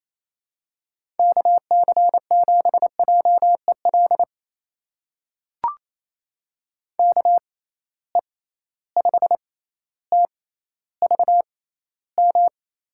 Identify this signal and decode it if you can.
Morse code, heard as a keyed tone, about 28 wpm, spaced as for 14 wpm, 700 Hz. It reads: KC7JEL KE5TVM
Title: USA Call Signs - 6 Characters - 6x Character Spacing - 28wpm